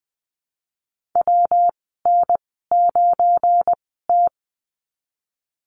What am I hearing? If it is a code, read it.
Morse code, heard as a keyed tone, about 20 wpm, spaced as for 15 wpm, 700 Hz. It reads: WN9T
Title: 186 Lesson 30g - Letter b - Call Signs 20wpm